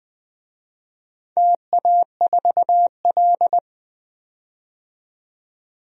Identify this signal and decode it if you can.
Morse code, heard as a keyed tone, about 20 wpm, 700 Hz. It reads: TA4L
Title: DXCC-100 Call Signs - 4 Characters - 20wpm